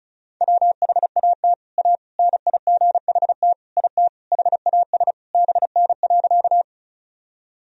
Morse code, heard as a keyed tone, about 36 wpm, 700 Hz. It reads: WHAT A NIGHT IT HAS BN.
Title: Sentences from Top 300 Words and Abbreviations 35wpm